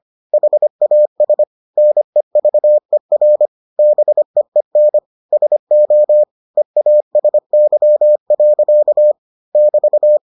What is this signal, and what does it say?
Morse code, heard as a keyed tone, about 25 wpm, 600 Hz. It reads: HAS NEVER BEEN SO EASY. <BT>